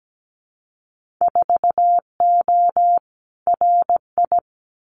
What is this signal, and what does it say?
Morse code, heard as a keyed tone, about 17 wpm, 700 Hz. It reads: 4O RI
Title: Winter Field Day Exchanges Encore 17wpm